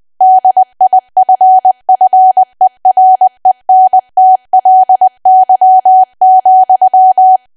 Morse code, heard as a keyed tone, about 20 wpm, 750 Hz. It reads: DIFFERENTLY,
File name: cwa / W1AW-Level3 / 20WPM